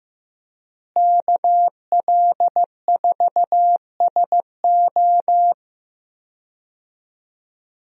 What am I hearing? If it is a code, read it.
Morse code, heard as a keyed tone, about 15 wpm, 700 Hz. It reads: KL4SO
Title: USA Call Signs - 5 Characters - Repeated 3x - 15wpm